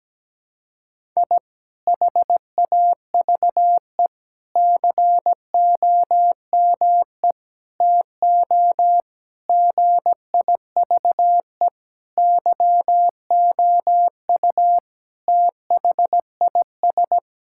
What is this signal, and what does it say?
Morse code, heard as a keyed tone, about 17 wpm, 700 Hz. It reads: I HAVE COME TO GIVE YOU THIS
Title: Sentences from Top 1000 Words - Review Part 6 - 17wpm